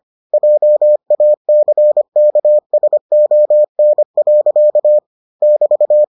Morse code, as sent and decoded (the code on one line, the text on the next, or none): .--- .- -.-. -.- ... --- -. .-.-.- / -...-
JACKSON. <BT>